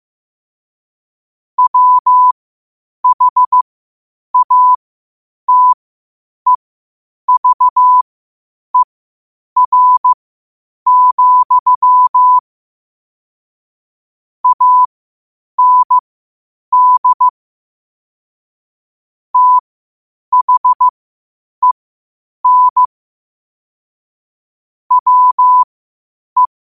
Morse code, heard as a keyed tone, about 15 wpm, spaced as for 8 wpm, 1000 Hz. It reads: WHATEVER, AND THEN WE
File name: qotdfiles / 8wpm